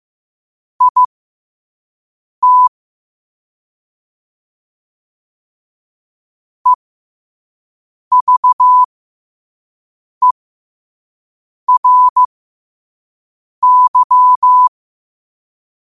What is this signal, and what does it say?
Morse code, heard as a keyed tone, about 15 wpm, spaced as for 5 wpm, 1000 Hz. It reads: IT EVERY
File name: 5wpm